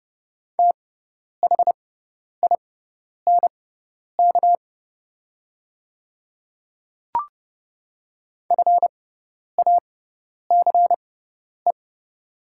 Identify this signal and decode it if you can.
Morse code, heard as a keyed tone, about 30 wpm, spaced as for 10 wpm, 700 Hz. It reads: THINK FACE
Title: 6x Character Spacing - Top 500 Words - 30wpm